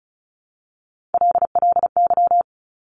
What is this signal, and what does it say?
Morse code, heard as a keyed tone, about 36 wpm, 700 Hz. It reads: LLY
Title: Top Three Letter Combinations in Words 35wpm